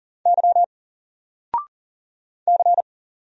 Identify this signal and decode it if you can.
Morse code, heard as a keyed tone, about 40 wpm, 700 Hz. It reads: Y C
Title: Single Letter-Number 40wpm